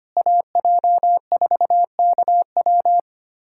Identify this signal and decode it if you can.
Morse code, heard as a keyed tone, about 25 wpm, 700 Hz. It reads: AJ4KW
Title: USA Callsign DE Callsign - 5 Characters - 25wpm